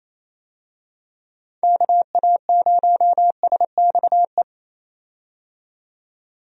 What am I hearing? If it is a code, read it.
Morse code, heard as a keyed tone, about 28 wpm, 700 Hz. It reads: KA0SXE
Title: USA Call Signs - 6 Characters - 28wpm